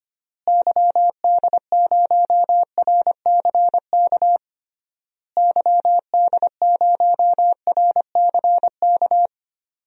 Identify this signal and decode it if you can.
Morse code, heard as a keyed tone, about 25 wpm, 700 Hz. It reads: YD0RCK YD0RCK
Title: DXCC-100 Call Signs - 6 Characters - Repeated 3x - 25wpm